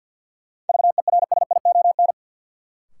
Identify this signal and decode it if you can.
Morse code, heard as a keyed tone, about 49 wpm, 700 Hz. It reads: VERSION
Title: Sentences from Top 1400 Words - New Word Drill Alt - 50wpm